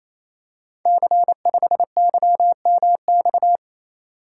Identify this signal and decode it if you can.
Morse code, heard as a keyed tone, about 27 wpm, 690 Hz. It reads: C5YMX
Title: Code Groups - 5 Letter:Numbers - 28wpm